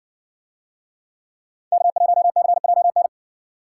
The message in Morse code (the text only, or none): K1CJN